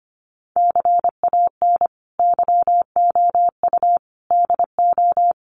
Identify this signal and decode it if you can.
Morse code, heard as a keyed tone, about 25 wpm, 700 Hz. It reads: CAN YOU DO
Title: Sentences from Top 100 Words 25wpm